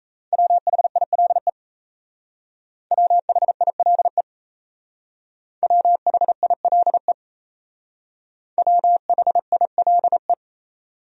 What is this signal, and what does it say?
Morse code, heard as a keyed tone, about 34 wpm, 700 Hz. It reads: WHILE WHILE WHILE WHILE
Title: Speed-Racing Top 300 Words 28wpm